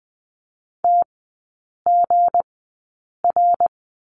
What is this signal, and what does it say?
Morse code, heard as a keyed tone, about 20 wpm, spaced as for 8 wpm, 700 Hz. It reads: TGR
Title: 218 Lesson 35e - Number 0 - Three Characters 20wpm